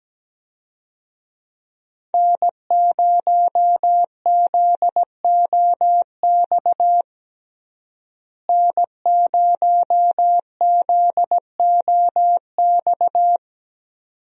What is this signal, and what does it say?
Morse code, heard as a keyed tone, about 17 wpm, 700 Hz. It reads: N0ZOX N0ZOX
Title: USA Call Signs - 5 Characters - Repeated 2x - 17wpm